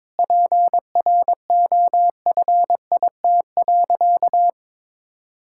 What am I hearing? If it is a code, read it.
Morse code, heard as a keyed tone, about 22 wpm, 700 Hz. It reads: PROFIT.
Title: Sentences from Top 1000 Words - Review Part 5 - 22wpm